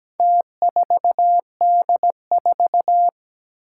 Morse code, heard as a keyed tone, about 17 wpm, 700 Hz. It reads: T4D4
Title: Four Challenging Letter-Number Combinations 17wpm